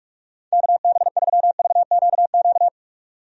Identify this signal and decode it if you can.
Morse code, heard as a keyed tone, about 42 wpm, 700 Hz. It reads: KB3VQQ
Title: USA Call Signs - 6 Characters - 45wpm